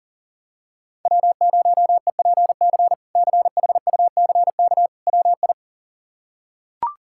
Morse code, heard as a keyed tone, about 40 wpm, 700 Hz. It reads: W0EPC CHUCK WI
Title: K1USN SST QSO Practice - 40wpm